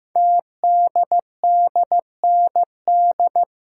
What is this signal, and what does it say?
Morse code, heard as a keyed tone, about 15 wpm, 700 Hz. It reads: TDDND
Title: Code Groups - 5 Letters - 15wpm